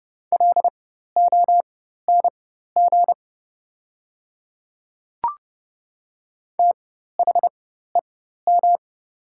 Morse code, heard as a keyed tone, about 31 wpm, spaced as for 14 wpm, 700 Hz. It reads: LONG THEM
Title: 4x Character Spacing - Top 100 Words - 30wpm